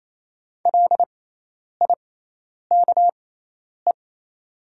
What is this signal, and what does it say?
Morse code, heard as a keyed tone, about 28 wpm, spaced as for 10 wpm, 700 Hz. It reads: LIKE
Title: 6x Character Spacing - Top 100 Words - 28wpm